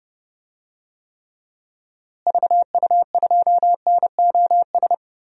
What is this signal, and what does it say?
Morse code, heard as a keyed tone, about 30 wpm, 700 Hz. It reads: VU2NOS
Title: DXCC-100 Call Signs - 6 Characters - Repeated 2x - 30wpm